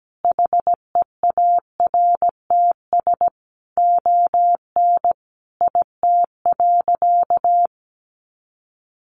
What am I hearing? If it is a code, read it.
Morse code, heard as a keyed tone, about 17 wpm, 700 Hz. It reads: HEARTS ON IT.
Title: Additional Easy Phrases 17wpm